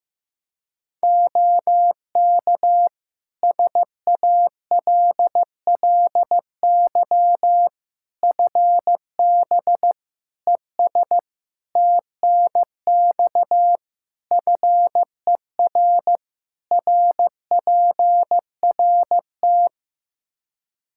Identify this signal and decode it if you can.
Morse code, heard as a keyed tone, about 16 wpm, 700 Hz. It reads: OK SALLY FB ES TNX FER RPRT
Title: Practice QSO - Exchange 2 - Info - 15wpm